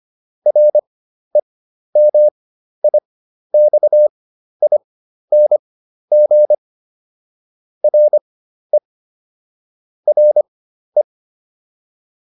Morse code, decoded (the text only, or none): REMIXING RE RE